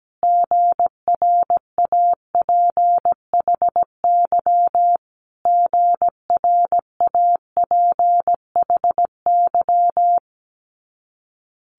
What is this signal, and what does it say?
Morse code, heard as a keyed tone, about 17 wpm, 700 Hz. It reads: GRAPHY GRAPHY
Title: Multi-letter Phonemes Repeated 3x 17wpm